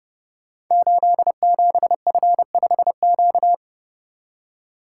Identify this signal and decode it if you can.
Morse code, heard as a keyed tone, about 30 wpm, 700 Hz. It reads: 87F5Q
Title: Code Groups - 5 Letter:Numbers - 30wpm